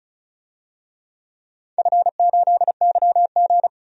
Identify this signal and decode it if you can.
Morse code, heard as a keyed tone, about 36 wpm, 700 Hz. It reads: F8YG
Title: DXCC-100 Call Signs - 4 Characters - 35wpm